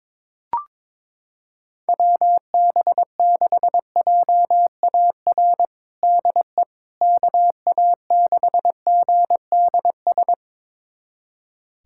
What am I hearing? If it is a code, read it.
Morse code, heard as a keyed tone, about 22 wpm, 700 Hz. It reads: WB6JAR DE KA6GDS
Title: USA Callsign DE Callsign - 6 Characters - 22wpm